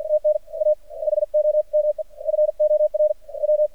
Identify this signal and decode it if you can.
Morse code, heard as a keyed tone, about 50 wpm, spaced as for 40 wpm, 610 Hz. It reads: TNA HOM FOR W
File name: archived / CRSnet